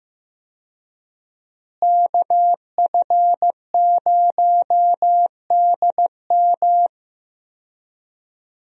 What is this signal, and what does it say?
Morse code, heard as a keyed tone, about 15 wpm, 700 Hz. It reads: KF0DM
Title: USA Call Signs - 5 Characters - Repeated 2x - 15wpm